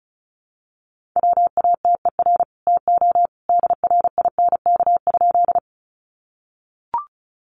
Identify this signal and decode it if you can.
Morse code, heard as a keyed tone, about 34 wpm, 700 Hz. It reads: WATER TO DRINK?
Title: Sentences from Top 1500 Words 35wpm